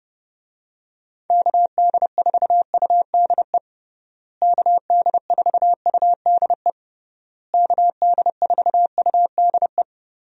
Morse code, heard as a keyed tone, about 30 wpm, 700 Hz. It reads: KD4UDE KD4UDE KD4UDE
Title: USA Call Signs - 6 Characters - Repeated 3x - 30wpm